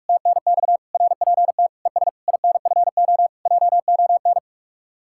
Morse code, heard as a keyed tone, about 59 wpm, spaced as for 52 wpm, 700 Hz. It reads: TNX RPT ES INFO JON
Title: Speed-Racing Common QSO Phrases Encore - 45wpm